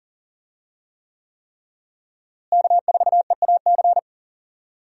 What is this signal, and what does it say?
Morse code, heard as a keyed tone, about 39 wpm, 700 Hz. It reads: K4EAC
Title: USA Call Signs - 5 Characters - Repeated 3x - 40wpm